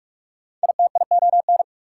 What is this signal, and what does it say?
Morse code, heard as a keyed tone, about 47 wpm, 690 Hz. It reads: ITION
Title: Prefixes and Suffixes 45wpm